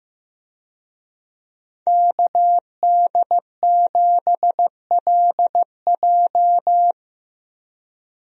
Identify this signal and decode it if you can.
Morse code, heard as a keyed tone, about 15 wpm, 700 Hz. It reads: KD7LJ
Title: USA Call Signs - 5 Characters - Repeated 3x - 15wpm